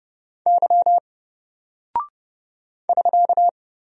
Morse code, heard as a keyed tone, about 30 wpm, 700 Hz. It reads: Y <SK>
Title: Single Character (Letter:Number:Punctuation:Prosign) Alt 30wpm